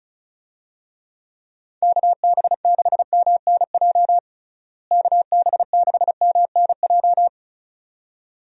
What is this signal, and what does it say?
Morse code, heard as a keyed tone, about 36 wpm, 700 Hz. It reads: KB6MNJ KB6MNJ
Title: USA Call Signs - 6 Characters - Repeated 2x - 35wpm